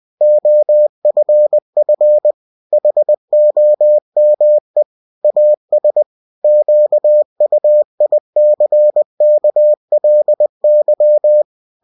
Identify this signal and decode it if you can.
Morse code, heard as a keyed tone, about 20 wpm, 600 Hz. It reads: OFF HOME AS QUICKLY
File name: cw-books / winnie-the-pooh / winnie-20wpm-page-0024